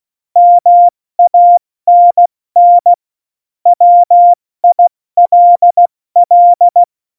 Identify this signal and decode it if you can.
Morse code, heard as a keyed tone, about 16 wpm, spaced as for 14 wpm, 700 Hz. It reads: MANN WILL
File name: Morsekurs / Witze / Witz 15. Oktober 2019, 70 F80